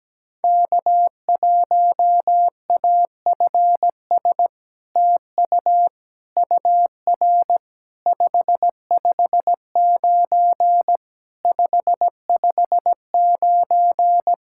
Morse code, heard as a keyed tone, about 17 wpm, 700 Hz. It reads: K1AFS TU UR 559 559
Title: SOTA QSO Practice 17wpm